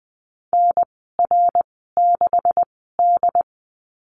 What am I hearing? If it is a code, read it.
Morse code, heard as a keyed tone, about 20 wpm, spaced as for 15 wpm, 700 Hz. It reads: NR6D
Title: 128 Lesson 21g - Number 6 - Call Signs 20wpm